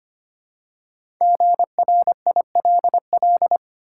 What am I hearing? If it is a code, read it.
Morse code, heard as a keyed tone, about 26 wpm, 690 Hz. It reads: GRILL
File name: practice / 25wpm / Easy Words with Hints - 25wpm